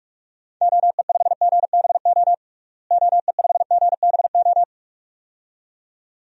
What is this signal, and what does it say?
Morse code, heard as a keyed tone, about 47 wpm, 700 Hz. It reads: OE5GBO OE5GBO
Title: DXCC-100 Call Signs - 6 Characters - Repeated 2x - 45wpm